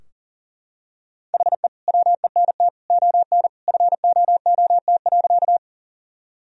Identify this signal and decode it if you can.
Morse code, heard as a keyed tone, about 37 wpm, 700 Hz. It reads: HE WENT ON FOOT.
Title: Sentences from Top 1000 Words - Review Part 3 - 40wpm